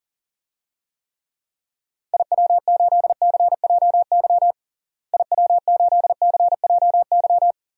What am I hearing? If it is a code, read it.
Morse code, heard as a keyed tone, about 40 wpm, 700 Hz. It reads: IW8CJY IW8CJY
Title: DXCC-100 Call Signs - 6 Characters - Repeated 2x - 40wpm